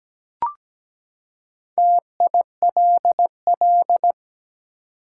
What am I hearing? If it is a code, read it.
Morse code, heard as a keyed tone, about 17 wpm, 700 Hz. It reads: TILL
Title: Four Letter Words - 17wpm